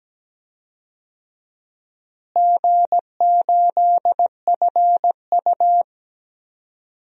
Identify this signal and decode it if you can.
Morse code, heard as a keyed tone, about 17 wpm, 700 Hz. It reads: G8FU
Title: DXCC-100 Call Signs - 4 Characters - Repeated 3x - 17wpm